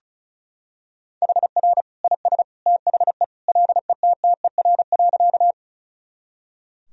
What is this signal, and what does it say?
Morse code, heard as a keyed tone, about 34 wpm, 700 Hz. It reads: HR IS THE LETTER.
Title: Sentences from Top 300 Words and Abbreviations 35wpm